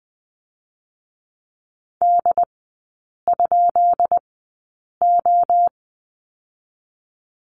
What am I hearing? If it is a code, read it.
Morse code, heard as a keyed tone, about 20 wpm, 700 Hz. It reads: D ? O
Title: 133 Lesson 22e - Question Mark - Three Characters 20wpm